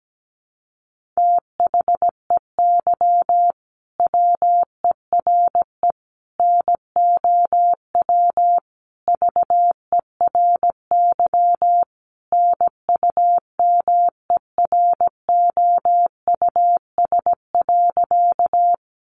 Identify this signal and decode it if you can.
Morse code, heard as a keyed tone, about 17 wpm, 700 Hz. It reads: THEY WERE NOW VERY NUMEROUS.
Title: Sentences from Top 1400 Words 17wpm